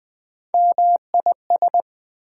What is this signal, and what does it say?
Morse code, heard as a keyed tone, about 20 wpm, 700 Hz. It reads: MIS
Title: Prefixes and Suffixes 20wpm